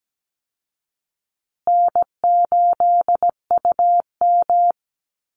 Morse code, heard as a keyed tone, about 17 wpm, 700 Hz. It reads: N8UM
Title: Easy Call Signs Repeated 3x - 17wpm